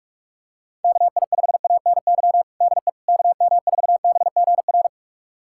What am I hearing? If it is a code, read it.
Morse code, heard as a keyed tone, about 42 wpm, 700 Hz. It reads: KI5ANY DE KM4BGR
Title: USA Callsign DE Callsign 45wpm